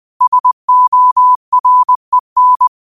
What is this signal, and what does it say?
Morse code, heard as a keyed tone, about 20 wpm, 1000 Hz. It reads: SOREN